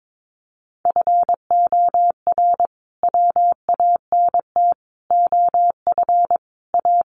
Morse code, heard as a keyed tone, about 22 wpm, 700 Hz. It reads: FOR WANT OF A
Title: A Sound Of Thunder 22wpm